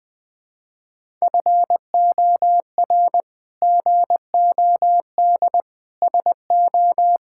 Morse code, heard as a keyed tone, about 20 wpm, 700 Hz. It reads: FOR GOD SO
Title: Sentences with 8-Word Phrases - 20wpm